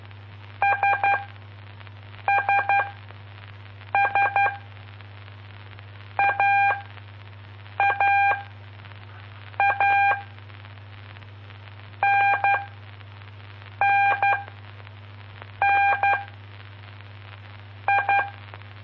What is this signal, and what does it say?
Morse code, heard as a keyed tone, about 12 wpm, 800 Hz. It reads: S S S A A A N N N I
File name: pp5bar / aulas de telegrafia / Aula-04